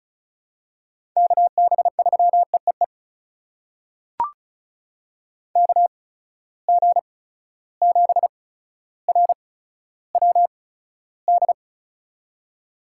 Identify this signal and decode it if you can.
Morse code, heard as a keyed tone, about 36 wpm, spaced as for 15 wpm, 700 Hz. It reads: KB3EEE KG7RWD
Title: USA Call Signs - 6 Characters - 8x Character Spacing - 35wpm